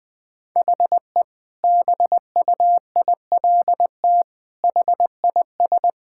Morse code, heard as a keyed tone, about 20 wpm, 700 Hz. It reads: HE BUILT HIS
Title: Sentences with 2-Word Phrases - Spoken Beginning - 20wpm